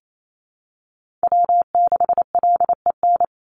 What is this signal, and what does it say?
Morse code, heard as a keyed tone, about 28 wpm, 700 Hz. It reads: W6LEN
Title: Call Signs - 28wpm